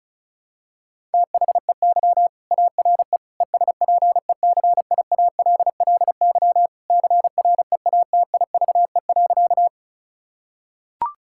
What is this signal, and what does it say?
Morse code, heard as a keyed tone, about 34 wpm, 700 Hz. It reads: THEY ARE ESPECIALLY CREATIVE.
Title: Sentences from Top 1000 Words - Review Part 5 - 35wpm